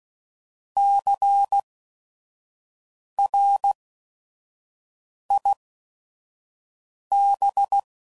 Morse code, heard as a keyed tone, about 16 wpm, spaced as for 5 wpm, 800 Hz. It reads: CRIB
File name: Prefixes / By First Letter / Cs